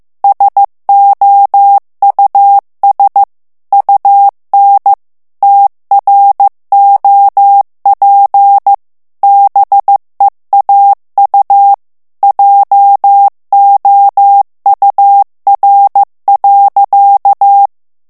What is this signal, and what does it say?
Morse code, heard as a keyed tone, about 15 wpm, 800 Hz. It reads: SOUS UN TROP BEAU JOUR.
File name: morse mp3 / FRANCAIS TEXTE CLAIR 1 VITESSE 14 MOTS MINUTES